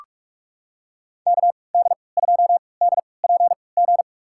Morse code, heard as a keyed tone, about 45 wpm, spaced as for 25 wpm, 690 Hz. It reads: KD2DPG
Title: USA Call Signs - 6 Characters - 3x Character Spacing - 45wpm